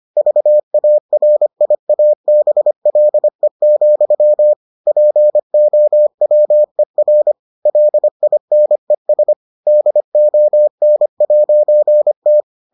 Morse code, heard as a keyed tone, about 25 wpm, 600 Hz. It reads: VARIABLE, POWER LINES DON'T